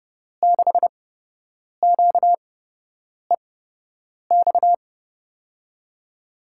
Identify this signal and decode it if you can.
Morse code, heard as a keyed tone, about 30 wpm, spaced as for 8 wpm, 700 Hz. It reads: 6QEX
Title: USA Call Signs - 6 Characters - 8x Character Spacing - 30wpm